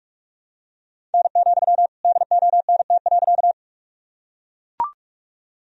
Morse code, heard as a keyed tone, about 41 wpm, 700 Hz. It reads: N, DONT.
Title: Sentences from Top 300 Words and Abbreviations 45wpm